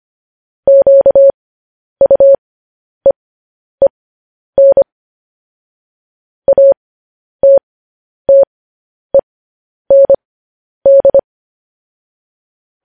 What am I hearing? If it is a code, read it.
Morse code, heard as a keyed tone, about 25 wpm, spaced as for 10 wpm, 550 Hz. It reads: QUEEN ATTEND
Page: News Headlines in Morse Code at 10 WPM